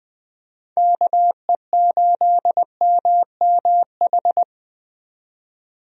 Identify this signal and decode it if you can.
Morse code, heard as a keyed tone, about 20 wpm, 700 Hz. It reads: KE8MMH
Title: USA Call Signs - 6 Characters - 20wpm